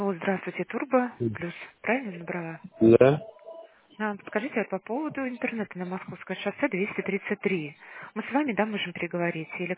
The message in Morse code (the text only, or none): none